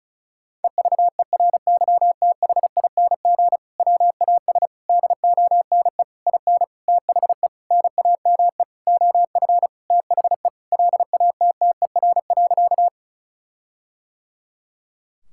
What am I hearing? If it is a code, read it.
Morse code, heard as a keyed tone, about 36 wpm, 700 Hz. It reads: EVERYTHING WAS DONE IN THE NAME OF THE LATTER.
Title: Sentences from Top 1600 Words 35wpm